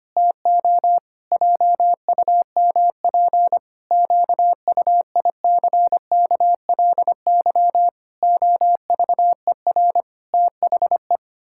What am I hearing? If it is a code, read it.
Morse code, heard as a keyed tone, about 25 wpm, 700 Hz. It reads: TO JUMP QUICKLY OVER THE
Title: Pangram Challenge v2 - 25wpm